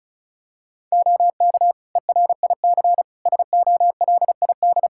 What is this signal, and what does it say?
Morse code, heard as a keyed tone, about 34 wpm, 700 Hz. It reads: OK ERIC SOLID
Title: Practice QSO - Exchange 3 - More Info - 35wpm